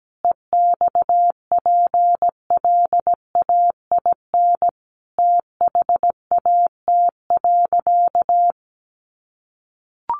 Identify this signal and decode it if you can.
Morse code, heard as a keyed tone, about 17 wpm, 700 Hz. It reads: EXPLAIN THAT.
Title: Sentences from Top 1100 Words 17wpm